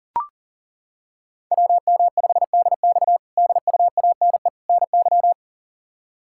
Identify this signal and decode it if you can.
Morse code, heard as a keyed tone, about 40 wpm, 700 Hz. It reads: WM5DX DUANE NY
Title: K1USN SST QSO Practice - 40wpm